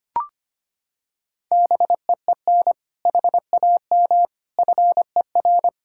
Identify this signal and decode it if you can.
Morse code, heard as a keyed tone, about 25 wpm, 700 Hz. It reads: BEEN HAM FER
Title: Practice QSO - Exchange 3 - More Info - 25wpm